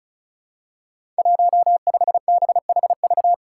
Morse code, heard as a keyed tone, about 36 wpm, 700 Hz. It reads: 15BHV
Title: Five Challenging Letter-Number Combinations 35wpm